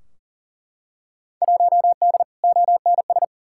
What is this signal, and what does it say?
Morse code, heard as a keyed tone, about 40 wpm, 700 Hz. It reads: 1D ONS
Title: Field Day Exchanges - 40wpm